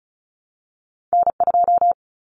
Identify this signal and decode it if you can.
Morse code, heard as a keyed tone, about 36 wpm, 700 Hz. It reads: N2
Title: Common Cut Numbers - 2-Digit 35wpm